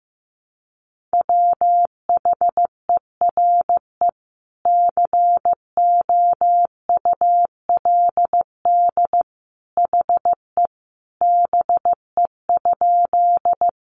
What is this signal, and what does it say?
Morse code, heard as a keyed tone, about 15 wpm, 700 Hz. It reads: WHERE COULD HE BE?
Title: Sentences from Top 200 Words 15wpm